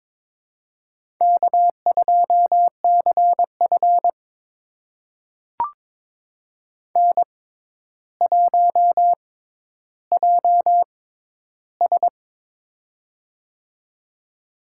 Morse code, heard as a keyed tone, about 22 wpm, spaced as for 11 wpm, 700 Hz. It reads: K2CF N1JS
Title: USA Call Signs - 4 Characters - 6x Character Spacing - 22wpm